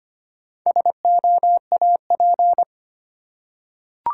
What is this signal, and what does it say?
Morse code, heard as a keyed tone, about 25 wpm, 700 Hz. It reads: SOAP
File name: practice / 25wpm / Difficult Words with Hints - 25wpm